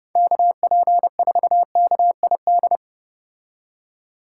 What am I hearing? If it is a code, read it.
Morse code, heard as a keyed tone, about 30 wpm, 700 Hz. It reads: KP4KID